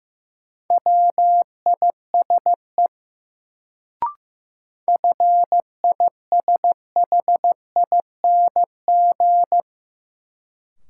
Morse code, heard as a keyed tone, about 15 wpm, 700 Hz. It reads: WISE FISHING
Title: Sentences from Top 1600 Words - New Word Drill Alt - 15wpm